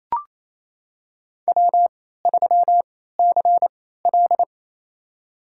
Morse code, heard as a keyed tone, about 28 wpm, spaced as for 16 wpm, 700 Hz. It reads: W3CL
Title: USA Call Signs - 4 Characters - 3x Character Spacing - 28wpm